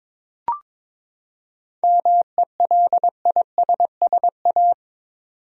Morse code, heard as a keyed tone, about 22 wpm, 700 Hz. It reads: MELISSA
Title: USA Common Names - 22wpm